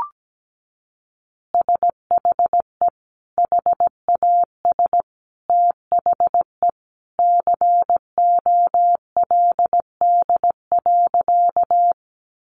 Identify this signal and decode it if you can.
Morse code, heard as a keyed tone, about 17 wpm, 700 Hz. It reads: SHE HAS THE COLD.
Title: Easy Phrases 17wpm